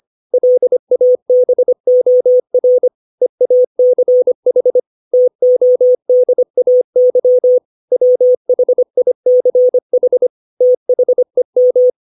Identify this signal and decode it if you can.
Morse code, heard as a keyed tone, about 25 wpm, 500 Hz. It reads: LABOR EACH TODAY WHICH THEM